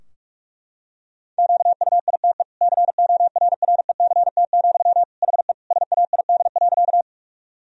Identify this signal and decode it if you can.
Morse code, heard as a keyed tone, about 46 wpm, 700 Hz. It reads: QUITE CORRECT, HE SAID.
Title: Sentences from Top 1000 Words - Review Part 2 - 45wpm